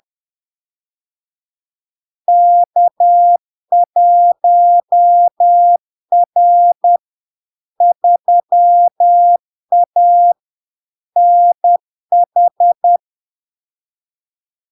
Morse code, heard as a keyed tone, about 10 wpm, 700 Hz. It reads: K1R 3A NH
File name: LARK-FD-2022-Log-in-CW-10wpm-700-Hz